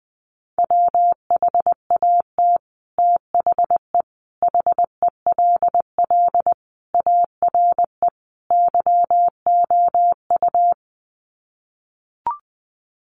Morse code, heard as a keyed tone, about 20 wpm, 700 Hz. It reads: WHAT THE HELL ARE YOU
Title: Sets of 5 Words Encore 20wpm